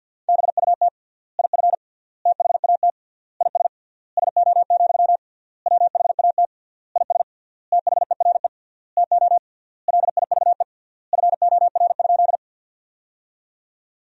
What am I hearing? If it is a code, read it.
Morse code, heard as a keyed tone, about 52 wpm, 700 Hz. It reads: BUT IF THAT IS SO, WHAT IS THERE TO LIVE FOR?
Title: 3x Word-Spacing - Sentences from Top 200 Words - 50wpm